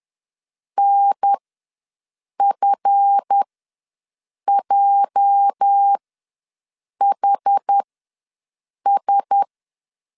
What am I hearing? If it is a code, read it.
Morse code, heard as a keyed tone, about 10 wpm, spaced as for 6 wpm, 800 Hz. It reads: NFJHS
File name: LEZIONI / 8 YJ45 / 8 YJ ws 10wpm 4043